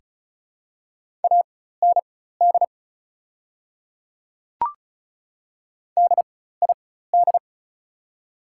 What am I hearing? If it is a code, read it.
Morse code, heard as a keyed tone, about 36 wpm, spaced as for 16 wpm, 690 Hz. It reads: AND DID
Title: 4x Character Spacing - Top 100 Words - 35wpm